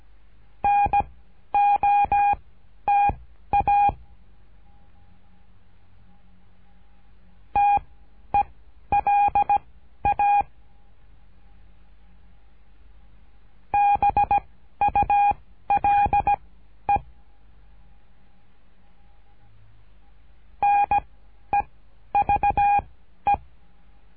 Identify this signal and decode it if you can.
Morse code, heard as a keyed tone, about 17 wpm, spaced as for 11 wpm, 810 Hz. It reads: NOTA TELA BULE NEVE
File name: cursos / CURSO DE TELEGRAFIA / Telegrafia - Trilha 13